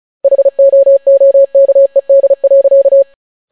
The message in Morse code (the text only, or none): HOOKED.